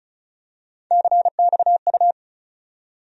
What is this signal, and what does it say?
Morse code, heard as a keyed tone, about 34 wpm, 700 Hz. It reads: CXU
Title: USA Call Sign Postfixes - 3 Letters 35wpm